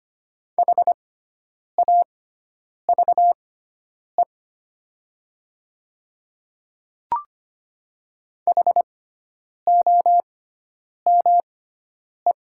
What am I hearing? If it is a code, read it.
Morse code, heard as a keyed tone, about 25 wpm, spaced as for 9 wpm, 700 Hz. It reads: HAVE HOME
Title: 6x Character Spacing - Top 200 Words - 25wpm